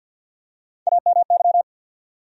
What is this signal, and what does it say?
Morse code, heard as a keyed tone, about 50 wpm, 700 Hz. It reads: AMY
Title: USA Common Names - Max 4 Letters - 50wpm